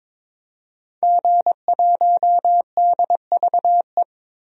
Morse code, heard as a keyed tone, about 22 wpm, 700 Hz. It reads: G1DVE
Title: DXCC-100 Call Signs - 5 Characters - 22wpm